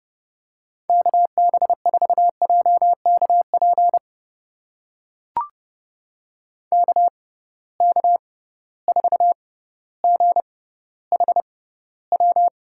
Morse code, heard as a keyed tone, about 30 wpm, spaced as for 15 wpm, 700 Hz. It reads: KB4JKP KK4GHW
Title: USA Call Signs - 6 Characters - 6x Character Spacing - 30wpm